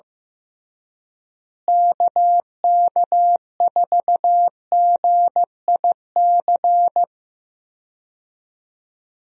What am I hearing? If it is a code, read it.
Morse code, heard as a keyed tone, about 16 wpm, 700 Hz. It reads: KK4GIC
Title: USA Call Signs - 6 Characters - Repeated 2x - 15wpm